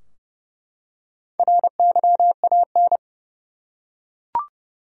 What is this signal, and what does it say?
Morse code, heard as a keyed tone, about 30 wpm, 700 Hz. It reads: RYAN
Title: USA Common Names - Max 5 Letters - 30wpm